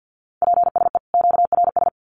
Morse code, heard as a keyed tone, about 50 wpm, 700 Hz. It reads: PSE QRS
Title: Common QSO Phrases 50wpm